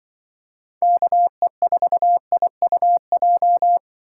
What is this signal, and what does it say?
Morse code, heard as a keyed tone, about 25 wpm, 700 Hz. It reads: KE4IUJ